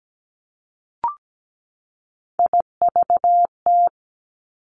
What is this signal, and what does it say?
Morse code, heard as a keyed tone, about 17 wpm, 700 Hz. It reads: IVT
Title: Three Challenging Letter-Number Combinations 17wpm